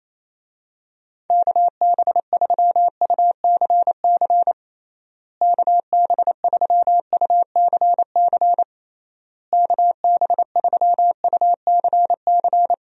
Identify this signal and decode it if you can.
Morse code, heard as a keyed tone, about 27 wpm, 700 Hz. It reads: KB3UCC KB3UCC KB3UCC
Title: USA Call Signs - 6 Characters - Repeated 3x - 28wpm